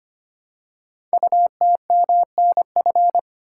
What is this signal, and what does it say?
Morse code, heard as a keyed tone, about 25 wpm, 700 Hz. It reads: UTMNF
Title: Code Groups - 5 Letters - 25wpm